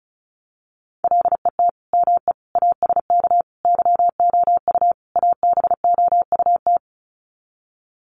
Morse code, heard as a keyed tone, about 34 wpm, 700 Hz. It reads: LET ME ASK YOU ABOUT
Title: Sets of 5 Words 35wpm